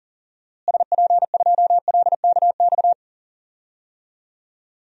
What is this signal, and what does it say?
Morse code, heard as a keyed tone, about 40 wpm, 700 Hz. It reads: SP2LKX
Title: DXCC-100 Call Signs - 6 Characters - Repeated 3x - 40wpm